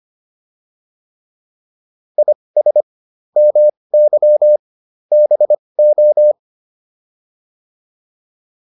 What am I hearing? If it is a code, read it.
Morse code, heard as a keyed tone, about 25 wpm, spaced as for 20 wpm, 600 Hz. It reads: IS MY BO
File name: PR103 20